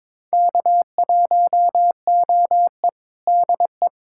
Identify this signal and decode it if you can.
Morse code, heard as a keyed tone, about 22 wpm, 700 Hz. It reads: K1OE DE
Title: USA Callsign DE Callsign - 4 Characters - 22wpm